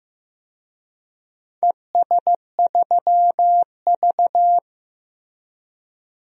Morse code, heard as a keyed tone, about 15 wpm, 700 Hz. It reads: ES3V
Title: DXCC-100 Call Signs - 4 Characters - Repeated 2x - 15wpm